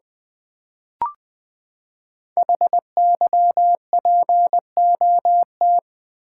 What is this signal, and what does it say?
Morse code, heard as a keyed tone, about 20 wpm, 700 Hz. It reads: HYPOT
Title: Build A Word - 5 Letters At A Time - 10-Letter Words - 20wpm